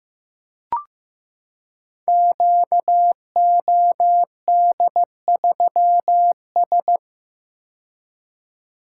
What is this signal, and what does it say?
Morse code, heard as a keyed tone, about 15 wpm, 700 Hz. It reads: QOD3S
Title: Code Groups - 5 Letter:Numbers - 15wpm